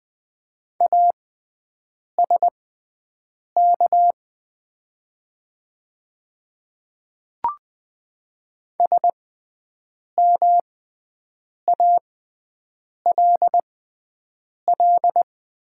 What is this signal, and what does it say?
Morse code, heard as a keyed tone, about 20 wpm, spaced as for 7 wpm, 700 Hz. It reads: ASK SMALL
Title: 6x Character Spacing - Top 200 Words - 20wpm